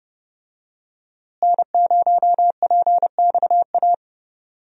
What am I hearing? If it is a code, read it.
Morse code, heard as a keyed tone, about 31 wpm, 700 Hz. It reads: N0PXA
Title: USA Call Signs - 5 Characters - Repeated 3x - 30wpm